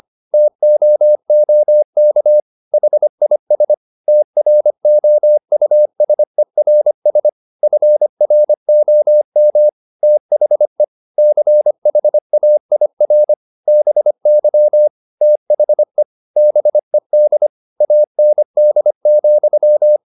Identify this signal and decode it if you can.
Morse code, heard as a keyed tone, about 25 wpm, 600 Hz. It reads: TOOK HIS TROUSERS FROM THE CHAIR BY THE BED AND,
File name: the old man and the sea part 1 25wpm